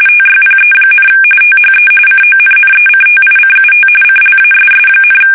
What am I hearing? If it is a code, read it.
RTTY, about 45.45 baud. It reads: TYPE, THE CONNECTICUT CLASS,
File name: fsk850 / battleships